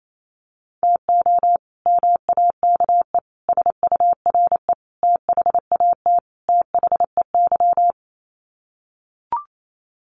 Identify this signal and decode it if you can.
Morse code, heard as a keyed tone, about 28 wpm, 700 Hz. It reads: TO MAKE SURE THAT THEY
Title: Sets of 5 Words Encore2 28wpm